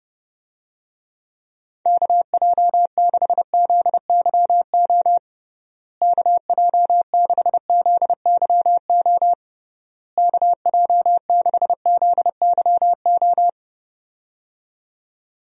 Morse code, heard as a keyed tone, about 29 wpm, 700 Hz. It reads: KJ6ZYO KJ6ZYO KJ6ZYO
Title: USA Call Signs - 6 Characters - Repeated 3x - 30wpm